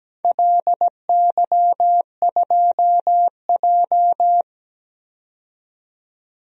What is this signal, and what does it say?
Morse code, heard as a keyed tone, about 17 wpm, 700 Hz. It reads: LY2J
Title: DXCC-100 Call Signs - 4 Characters - 17wpm